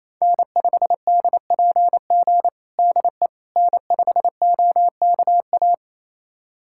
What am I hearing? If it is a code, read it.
Morse code, heard as a keyed tone, about 28 wpm, 700 Hz. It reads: N5DPG DE N5OKA
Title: USA Callsign DE Callsign - 5 Characters - 28wpm